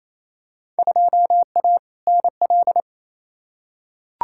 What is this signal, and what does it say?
Morse code, heard as a keyed tone, about 27 wpm, 700 Hz. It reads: 2A NL
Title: Field Day Exchanges Encore - 28wpm